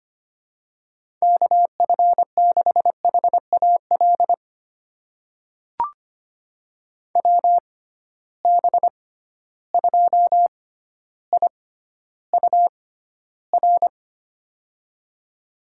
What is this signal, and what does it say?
Morse code, heard as a keyed tone, about 25 wpm, spaced as for 13 wpm, 700 Hz. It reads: KF6HAL WB2IUR
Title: USA Call Signs - 6 Characters - 6x Character Spacing - 25wpm